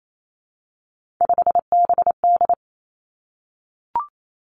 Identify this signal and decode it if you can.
Morse code, heard as a keyed tone, about 28 wpm, 700 Hz. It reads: 5BD
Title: Three Challenging Letter-Number Combinations 28wpm